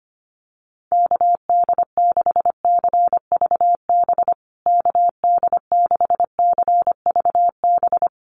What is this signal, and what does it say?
Morse code, heard as a keyed tone, about 25 wpm, 700 Hz. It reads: KD6CVB KD6CVB
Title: USA Call Signs Repeated 3x 25wpm